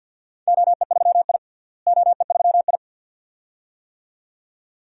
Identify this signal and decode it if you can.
Morse code, heard as a keyed tone, about 51 wpm, 700 Hz. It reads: OE3I OE3I
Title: DXCC-100 Call Signs - 4 Characters - Repeated 2x - 50wpm